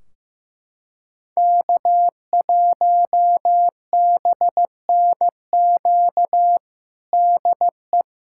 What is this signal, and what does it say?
Morse code, heard as a keyed tone, about 15 wpm, 700 Hz. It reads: K1BNQ DE
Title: K1USN SST QSO Practice 15wpm